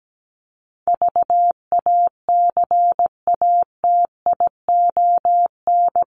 Morse code, heard as a keyed tone, about 17 wpm, 700 Hz. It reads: VACATION
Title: Top 1501-1750 Words 17wpm